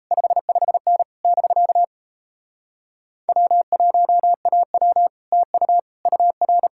Morse code, heard as a keyed tone, about 37 wpm, 700 Hz. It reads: 55N <BK> W1AW TU UR